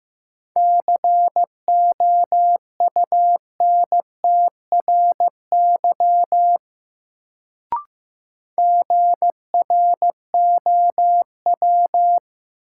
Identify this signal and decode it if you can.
Morse code, heard as a keyed tone, about 15 wpm, 700 Hz. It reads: COUNTRY GROW
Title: Sentences from Top 300 Words - New Word Drill - 15wpm